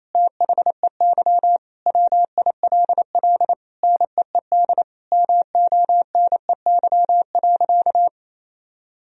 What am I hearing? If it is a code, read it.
Morse code, heard as a keyed tone, about 28 wpm, 700 Hz. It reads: THEY WILL NEED MONEY.
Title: Sentences from Top 1000 Words - Review Part 4 - 28wpm